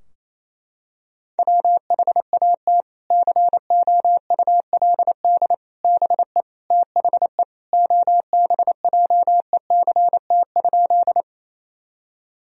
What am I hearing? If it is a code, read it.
Morse code, heard as a keyed tone, about 28 wpm, 700 Hz. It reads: WHAT COULD BE THE OBJECT?
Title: Sentences from Top 1000 Words - Review Part 3 - 28wpm